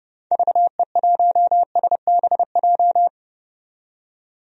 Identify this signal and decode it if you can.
Morse code, heard as a keyed tone, about 30 wpm, 700 Hz. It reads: VE1SBJ
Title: DXCC-100 Call Signs - 6 Characters - 30wpm